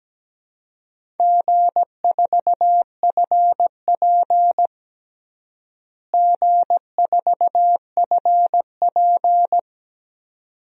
Morse code, heard as a keyed tone, about 17 wpm, 700 Hz. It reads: G4FP G4FP
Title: DXCC-100 Call Signs - 4 Characters - Repeated 2x - 17wpm